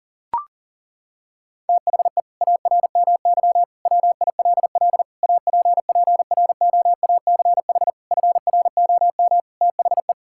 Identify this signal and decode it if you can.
Morse code, heard as a keyed tone, about 37 wpm, 700 Hz. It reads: THE ARMY WILL APPROACH FROM THE
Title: Sentences from Top 1000 Words - Review Part 6 - 40wpm